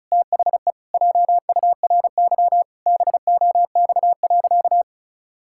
Morse code, heard as a keyed tone, about 36 wpm, 700 Hz. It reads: THE JURY BOX.
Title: Pangram Challenge v2 - 35wpm